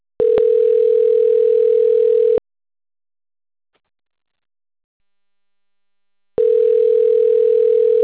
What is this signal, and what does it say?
Ringback tone (North America), 440 + 480 Hz.
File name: out- - - - - .13350